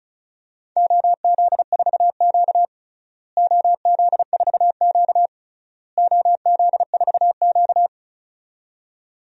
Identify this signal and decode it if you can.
Morse code, heard as a keyed tone, about 36 wpm, 700 Hz. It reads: OZ4Q OZ4Q OZ4Q
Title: DXCC-100 Call Signs - 4 Characters - Repeated 3x - 35wpm